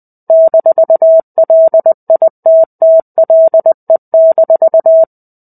Morse code, heard as a keyed tone, about 20 wpm, 650 Hz. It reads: -LITTLE-
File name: 05-Chapter-05 20 WPM